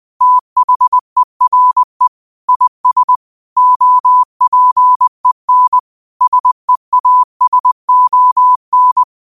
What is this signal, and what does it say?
Morse code, heard as a keyed tone, about 20 wpm, 1000 Hz. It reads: THERE IS OPEN SEASON
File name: qotdfiles / 20wpm